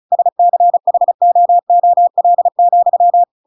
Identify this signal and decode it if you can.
Morse code, heard as a keyed tone, about 36 wpm, 700 Hz. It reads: SCHOOL,